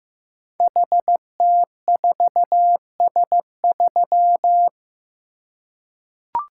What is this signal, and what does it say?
Morse code, heard as a keyed tone, about 15 wpm, 700 Hz. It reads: HT4S3
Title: Code Groups - 5 Letter:Numbers - 15wpm